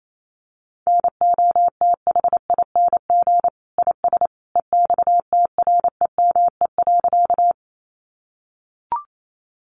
Morse code, heard as a keyed tone, about 28 wpm, 700 Hz. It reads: NOTHING IS EXTREME.
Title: Sentences from Top 1300 Words 28wpm